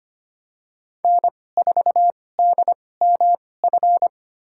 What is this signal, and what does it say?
Morse code, heard as a keyed tone, about 25 wpm, spaced as for 18 wpm, 700 Hz. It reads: N4DMF
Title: USA Call Signs - 5 Characters - 2x Character Spacing - 25wpm